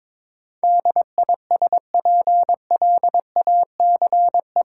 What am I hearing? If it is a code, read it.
Morse code, heard as a keyed tone, about 22 wpm, 700 Hz. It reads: DISPLACE
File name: practice / 22wpm / Eight Letter Words - 22wpm